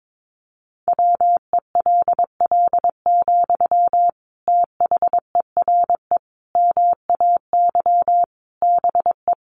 Morse code, heard as a keyed tone, about 22 wpm, 700 Hz. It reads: WELL, THERE MAY BE
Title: Sentences from Top 1300 Words 22wpm